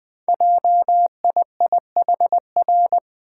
Morse code, heard as a keyed tone, about 20 wpm, 700 Hz. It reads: JIIHR
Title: Code Groups - 5 Letters - 20wpm